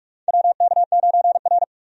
Morse code, heard as a keyed tone, about 44 wpm, 700 Hz. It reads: WK9R
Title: USA Call Signs - 4 Characters - 45wpm